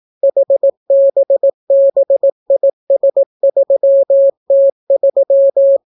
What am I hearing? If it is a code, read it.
Morse code, heard as a keyed tone, about 18 wpm, 550 Hz. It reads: HBBIS3T3